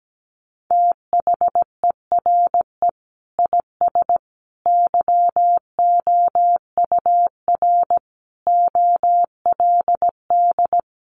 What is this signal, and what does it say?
Morse code, heard as a keyed tone, about 17 wpm, 700 Hz. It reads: THERE IS YOUR OLD
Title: Sentences from Top 500 Words 17wpm